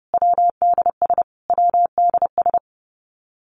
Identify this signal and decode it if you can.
Morse code, heard as a keyed tone, about 30 wpm, 700 Hz. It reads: WDS WDS
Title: Abbreviations Repeated 3x - 30wpm